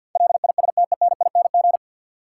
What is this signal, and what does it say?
Morse code, heard as a keyed tone, about 48 wpm, 700 Hz. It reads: LISTENING
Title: Sets of 5 Words - New Word Drill - 50wpm